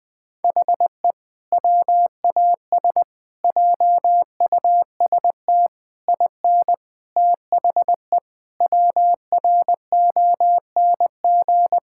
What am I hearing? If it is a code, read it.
Morse code, heard as a keyed tone, about 20 wpm, 700 Hz. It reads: HE WAS JUST IN THE WRONG
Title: Sentences with 7-Word Phrases - Spoken Beginning - 20wpm